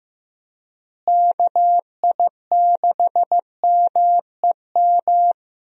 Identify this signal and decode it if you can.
Morse code, heard as a keyed tone, about 15 wpm, 700 Hz. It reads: KI6MEM
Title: USA Call Signs - 6 Characters - Repeated 3x - 15wpm